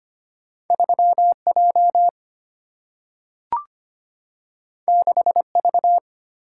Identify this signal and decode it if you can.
Morse code, heard as a keyed tone, about 25 wpm, 700 Hz. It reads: 3J 6V
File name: Two Letter-Number Combinations - 25wpm